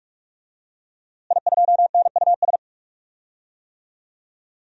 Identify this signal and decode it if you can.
Morse code, heard as a keyed tone, about 46 wpm, 700 Hz. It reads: I2NUS
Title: DXCC-100 Call Signs - 5 Characters - Repeated 2x - 45wpm